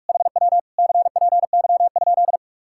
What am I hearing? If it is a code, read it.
Morse code, heard as a keyed tone, about 56 wpm, 700 Hz. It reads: HW CPY?
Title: Speed-Racing Common QSO Phrases Encore - 45wpm